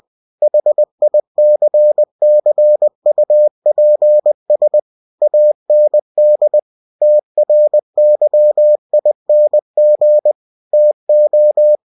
Morse code, heard as a keyed tone, about 20 wpm, 600 Hz. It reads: HICCUPS AND TRYING TO
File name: cw-books / winnie-the-pooh / winnie-20wpm-page-0107